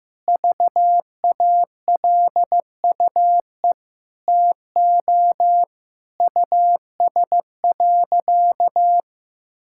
Morse code, heard as a keyed tone, about 15 wpm, 700 Hz. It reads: VALUE TO US.
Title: Sentences from Top 1000 Words - Review Part 5 - 15wpm